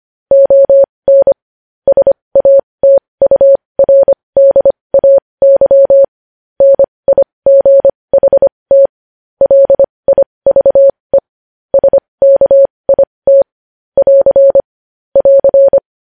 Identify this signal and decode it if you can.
Morse code, heard as a keyed tone, about 25 wpm, spaced as for 20 wpm, 550 Hz. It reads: ON SATURDAY NIGHT LIVE SKIT <AR> <AR>
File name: podcast / News headlines-1x-25x20